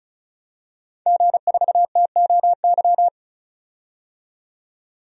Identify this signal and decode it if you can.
Morse code, heard as a keyed tone, about 34 wpm, 700 Hz. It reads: G4TOY
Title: DXCC-100 Call Signs - 5 Characters - 35wpm